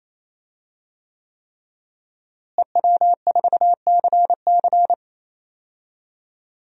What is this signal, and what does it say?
Morse code, heard as a keyed tone, about 28 wpm, 700 Hz. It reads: EW4CC
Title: DXCC-100 Call Signs - 5 Characters - Repeated 3x - 28wpm